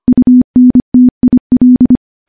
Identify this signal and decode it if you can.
Morse code, heard as a keyed tone, about 25 wpm, 250 Hz. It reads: UNTIL
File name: morse / morse-training-25WPM